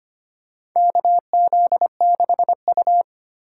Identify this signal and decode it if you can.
Morse code, heard as a keyed tone, about 25 wpm, 700 Hz. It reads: KZ6U
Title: USA Call Signs - 4 Characters - 4x Character Spacing - 25wpm